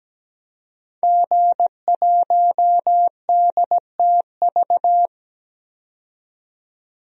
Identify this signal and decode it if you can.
Morse code, heard as a keyed tone, about 17 wpm, 700 Hz. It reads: G1DTV
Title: DXCC-100 Call Signs - 5 Characters - 17wpm